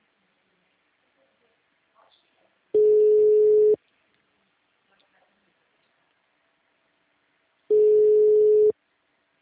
Ringback tone (Europe, 425 Hz).
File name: out- -2011- - - .266